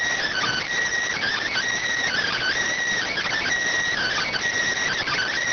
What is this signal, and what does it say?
Emergency Alert System (SAME) tones, heard as harsh buzzing bursts. It is heard from 0.3 to 4.3 s.